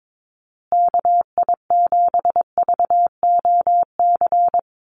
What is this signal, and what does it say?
Morse code, heard as a keyed tone, about 22 wpm, 700 Hz. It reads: KI7VOC
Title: USA Call Signs Repeated 2x 22wpm